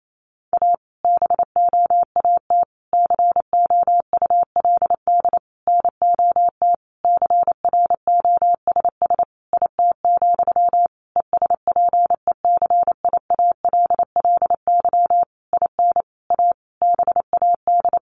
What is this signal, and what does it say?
Morse code, heard as a keyed tone, about 28 wpm, 700 Hz. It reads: A BOAT COULD NOT CROSS IT, ESPECIALLY IN A BAD
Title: Sentences from Top 1400 Words 28wpm